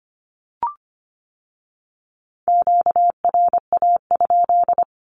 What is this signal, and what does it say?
Morse code, heard as a keyed tone, about 25 wpm, 700 Hz. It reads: QRA?
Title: Q-Codes with Spoken Meaning - 25wpm